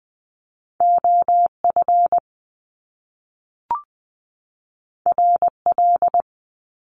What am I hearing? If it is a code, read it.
Morse code, heard as a keyed tone, about 20 wpm, 700 Hz. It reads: OF RL
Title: Top Two Letter Combinations in Words 20wpm